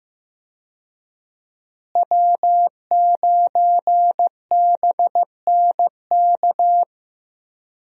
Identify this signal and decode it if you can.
Morse code, heard as a keyed tone, about 15 wpm, 700 Hz. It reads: W9BNK
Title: USA Call Signs - 5 Characters - Repeated 2x - 15wpm